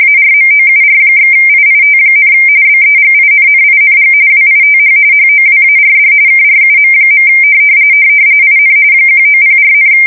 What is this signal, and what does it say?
RTTY, about 45.45 baud. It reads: ARMOUR/IIT, BEGINNING WITH HOWARD KRUM. IN 1904,
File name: fsk170 / teletype-corp